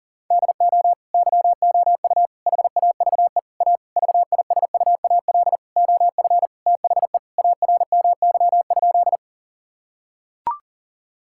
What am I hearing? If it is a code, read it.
Morse code, heard as a keyed tone, about 40 wpm, 700 Hz. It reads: DO YOU HAVE A VISUAL OF THE ARMY?
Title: Sentences from Top 1000 Words - Review Part 6 - 40wpm